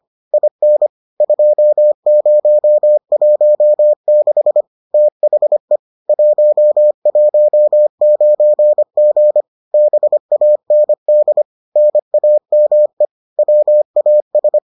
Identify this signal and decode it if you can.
Morse code, heard as a keyed tone, about 25 wpm, 600 Hz. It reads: IN 2016 THE 119G BAND NAME WAS